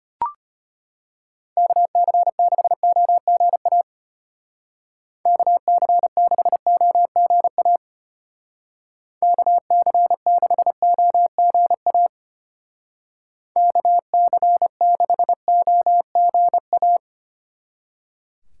Morse code, heard as a keyed tone, about 30 wpm, 700 Hz. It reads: KC6OGA KC6OGA KC6OGA KC6OGA
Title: Speed Racing - USA 2x3 Call Signs 25wpm